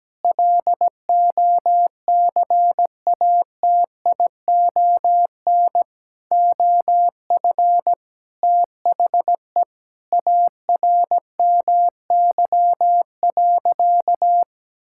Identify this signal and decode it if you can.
Morse code, heard as a keyed tone, about 17 wpm, 700 Hz. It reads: LOCATION OF THE ARMY.
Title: Sentences from Top 1000 Words - Review Part 6 - 17wpm